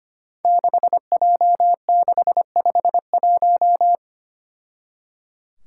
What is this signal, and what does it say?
Morse code, heard as a keyed tone, about 25 wpm, 700 Hz. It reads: 6J651
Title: Five Challenging Letter-Number Combinations 25wpm